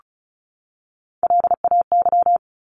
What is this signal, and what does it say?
Morse code, heard as a keyed tone, about 34 wpm, 700 Hz. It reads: LAY
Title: Top 500 Words 35wpm